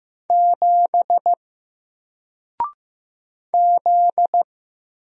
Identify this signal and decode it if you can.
Morse code, heard as a keyed tone, about 15 wpm, 700 Hz. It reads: 7 Z
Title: Single Character (Letter:Number:Punctuation:Prosign) 15wpm